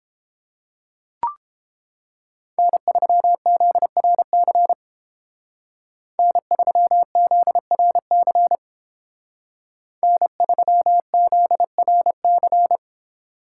Morse code, heard as a keyed tone, about 30 wpm, 700 Hz. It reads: N3ZRC N3ZRC N3ZRC